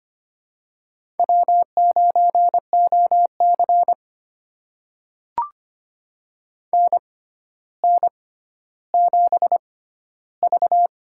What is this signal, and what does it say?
Morse code, heard as a keyed tone, about 25 wpm, spaced as for 13 wpm, 700 Hz. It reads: W9OC NN7V
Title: USA Call Signs - 6x Character Spacing - 25wpm